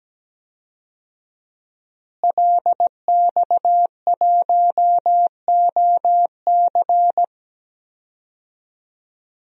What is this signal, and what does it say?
Morse code, heard as a keyed tone, about 17 wpm, 700 Hz. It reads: LX1OC
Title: DXCC-100 Call Signs - 5 Characters - Repeated 2x - 17wpm